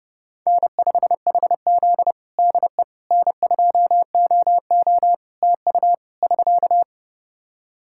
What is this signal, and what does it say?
Morse code, heard as a keyed tone, about 30 wpm, 700 Hz. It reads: N5HZ DE N2OO TU <SK>
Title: Practice QSO - Exchange 4 - Ending - 30wpm